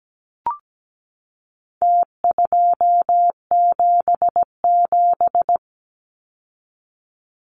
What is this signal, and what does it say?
Morse code, heard as a keyed tone, about 17 wpm, 700 Hz. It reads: T277
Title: Numbers - Contest Style - 4-Digit 17wpm